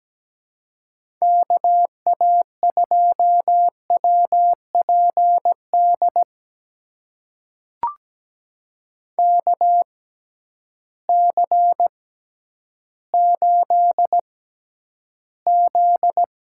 Morse code, heard as a keyed tone, about 17 wpm, spaced as for 10 wpm, 700 Hz. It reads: KA2WPD KC8Z
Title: USA Call Signs - 6 Characters - 6x Character Spacing - 17wpm